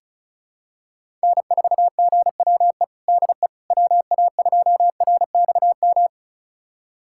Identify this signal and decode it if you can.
Morse code, heard as a keyed tone, about 36 wpm, 690 Hz. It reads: N4GWE DE WA2RXM
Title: USA Callsign DE Callsign 35wpm